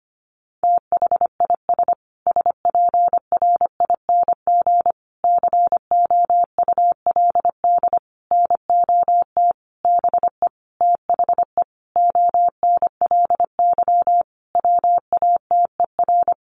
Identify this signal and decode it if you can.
Morse code, heard as a keyed tone, about 25 wpm, 700 Hz. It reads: THIS SPRING COULD NOT BE THE ONLY WATER
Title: Sentences from Top 1500 Words 25wpm